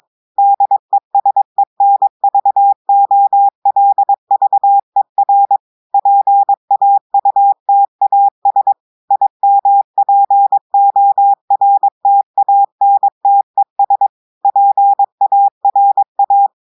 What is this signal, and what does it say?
Morse code, heard as a keyed tone, about 22 wpm, 800 Hz. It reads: DESENVOLVER PAUTAS IMPORTANTES PARA
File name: audios / portugais / portugais texte en clair 2 vitesse 22 mots minutes